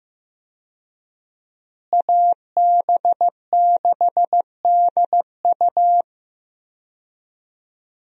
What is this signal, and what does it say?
Morse code, heard as a keyed tone, about 15 wpm, 700 Hz. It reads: AB6DU
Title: USA Call Signs - 5 Characters - Repeated 2x - 15wpm